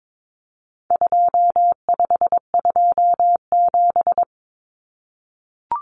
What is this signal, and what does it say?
Morse code, heard as a keyed tone, about 22 wpm, 700 Hz. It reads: 2527
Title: Numbers - 4-Digit 22wpm